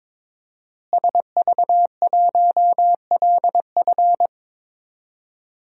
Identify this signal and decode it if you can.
Morse code, heard as a keyed tone, about 22 wpm, 700 Hz. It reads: SV1LF
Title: DXCC-100 Call Signs - 5 Characters - Repeated 3x - 22wpm